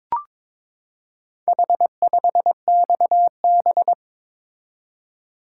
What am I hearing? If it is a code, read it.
Morse code, heard as a keyed tone, about 22 wpm, 700 Hz. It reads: H5XB
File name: practice / 22wpm / Four Letter-Number Combinations - 22wpm